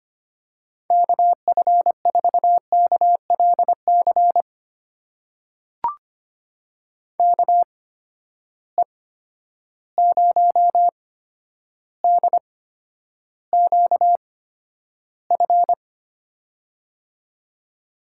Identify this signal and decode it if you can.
Morse code, heard as a keyed tone, about 25 wpm, spaced as for 11 wpm, 700 Hz. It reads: KF4KLC KE0DQF
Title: USA Call Signs - 6 Characters - 8x Character Spacing - 25wpm